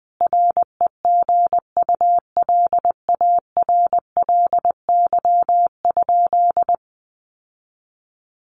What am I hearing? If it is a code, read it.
Morse code, heard as a keyed tone, about 20 wpm, 700 Hz. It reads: REGULARLY?
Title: Sentences from Top 1400 Words 20wpm